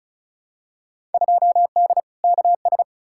Morse code, heard as a keyed tone, about 35 wpm, 700 Hz. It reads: 2D KS
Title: Field Day Exchanges Encore - 35wpm